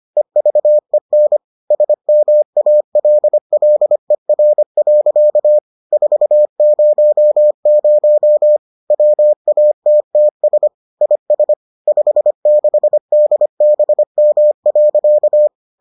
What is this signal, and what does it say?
Morse code, heard as a keyed tone, about 25 wpm, 600 Hz. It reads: EVEN SMALLER. 400 WATTS IS 56DBM.